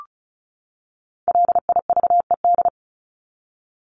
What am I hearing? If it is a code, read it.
Morse code, heard as a keyed tone, about 34 wpm, 700 Hz. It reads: LIVED
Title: ED Suffix - Top 100 Words with Base Words 35wpm